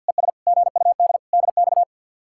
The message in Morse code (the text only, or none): ES GUD DX